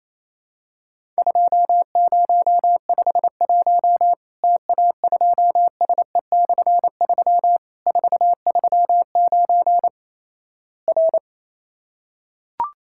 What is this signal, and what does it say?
Morse code, heard as a keyed tone, about 28 wpm, 700 Hz. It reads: 2051 TA2SE/3 439 R
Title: Worked All Europe DX Contest - QTC Exchange - 28wpm